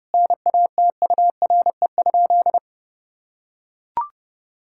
Morse code, heard as a keyed tone, about 30 wpm, 700 Hz. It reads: NATURE?
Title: Sentences from Top 1000 Words - Review Part 5 - 30wpm